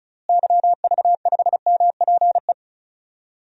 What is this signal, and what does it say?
Morse code, heard as a keyed tone, about 36 wpm, 700 Hz. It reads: YV5MPE
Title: DXCC-100 Call Signs - 6 Characters - Repeated 3x - 35wpm